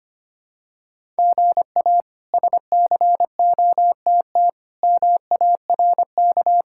Morse code, heard as a keyed tone, about 25 wpm, 700 Hz. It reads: GA SCOTT MARK
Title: K1USN SST QSO Practice - 25wpm